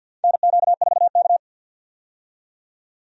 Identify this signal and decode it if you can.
Morse code, heard as a keyed tone, about 52 wpm, 700 Hz. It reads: NQ4K
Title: USA Call Signs - 4 Characters - 50wpm